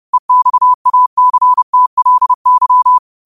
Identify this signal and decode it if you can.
Morse code, heard as a keyed tone, about 30 wpm, 1000 Hz. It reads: EXACTLY
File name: qotdfiles / 30wpm